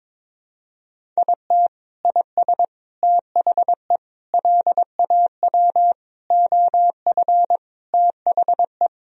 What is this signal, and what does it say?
Morse code, heard as a keyed tone, about 22 wpm, 700 Hz. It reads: IT IS THE LAW OF THE
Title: Sentences from Top 1000 Words - Review Part 3 - 22wpm